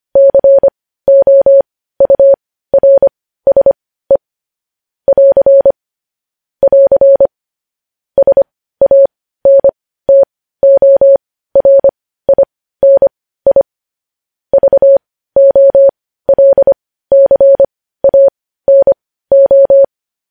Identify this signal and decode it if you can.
Morse code, heard as a keyed tone, about 25 wpm, spaced as for 15 wpm, 550 Hz. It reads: COURSE <AR> <AR> SANTORINI VOLCANO
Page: news headlines in morse code at 15 wpm